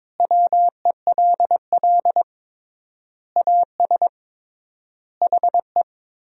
Morse code, heard as a keyed tone, about 22 wpm, 700 Hz. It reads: WELL AS HE
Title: 3x Word-Spacing - Sentences from Top 500 Words - 22wpm